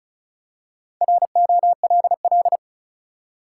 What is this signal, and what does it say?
Morse code, heard as a keyed tone, about 34 wpm, 700 Hz. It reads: ROLL
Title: Four Letter Words - 35wpm